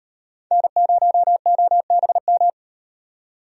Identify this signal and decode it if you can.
Morse code, heard as a keyed tone, about 36 wpm, 700 Hz. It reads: N0OBM